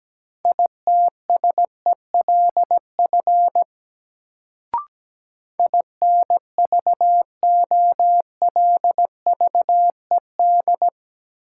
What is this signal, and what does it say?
Morse code, heard as a keyed tone, about 17 wpm, 700 Hz. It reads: ITSELF INVOLVED
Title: Sentences from Top 800 Words - New Word Drill - 17wpm